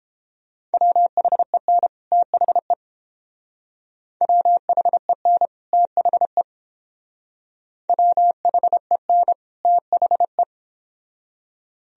Morse code, heard as a keyed tone, about 28 wpm, 700 Hz. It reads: WHEN THE WHEN THE WHEN THE